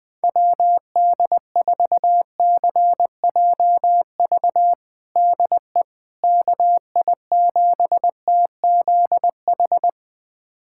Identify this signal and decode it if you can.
Morse code, heard as a keyed tone, about 21 wpm, 700 Hz. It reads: WD4CJV DE KI7TZH
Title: USA Callsign DE Callsign - 6 Characters - 20wpm